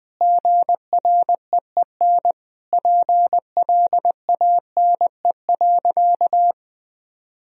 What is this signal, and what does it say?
Morse code, heard as a keyed tone, about 20 wpm, 700 Hz. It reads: GREEN PLANE.
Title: Sentences from Top 1000 Words - Review Part 3 - 20wpm